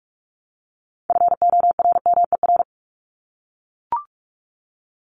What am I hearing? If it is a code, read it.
Morse code, heard as a keyed tone, about 44 wpm, 700 Hz. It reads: FORMER
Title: LY Suffix - Top 100 Words with Base Words 45wpm